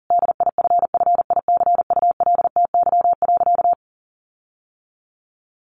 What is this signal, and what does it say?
Morse code, heard as a keyed tone, about 40 wpm, 700 Hz. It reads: DIFFICULTY.
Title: Sentences from Top 1500 Words 40wpm